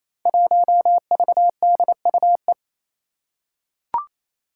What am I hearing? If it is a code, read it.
Morse code, heard as a keyed tone, about 28 wpm, 700 Hz. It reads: 1VDUE
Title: Five Challenging Letter-Number Combinations 28wpm